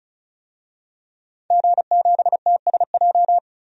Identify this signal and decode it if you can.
Morse code, heard as a keyed tone, about 36 wpm, 700 Hz. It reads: G7TSJ
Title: DXCC-100 Call Signs - 5 Characters - 35wpm